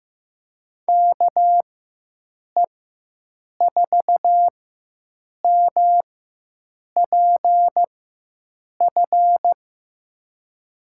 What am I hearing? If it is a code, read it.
Morse code, heard as a keyed tone, about 15 wpm, spaced as for 7 wpm, 700 Hz. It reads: KE4MPF
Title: USA Call Signs - 6 Characters - 4x Character Spacing - 15wpm